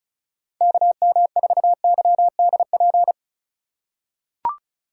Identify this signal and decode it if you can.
Morse code, heard as a keyed tone, about 36 wpm, 700 Hz. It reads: KM4YDP
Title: USA Call Signs - 4x Character Spacing - 35wpm